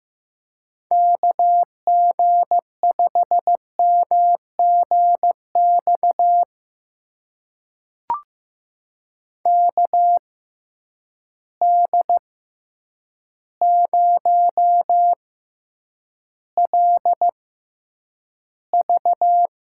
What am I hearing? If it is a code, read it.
Morse code, heard as a keyed tone, about 16 wpm, spaced as for 8 wpm, 700 Hz. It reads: KG5MGX KD0LV
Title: USA Call Signs - 6 Characters - 6x Character Spacing - 15wpm